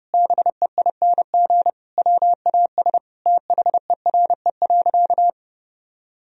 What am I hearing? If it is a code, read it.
Morse code, heard as a keyed tone, about 30 wpm, 700 Hz. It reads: BEING WAS THERE.
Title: Sentences from Top 1000 Words - Review Part 4 - 30wpm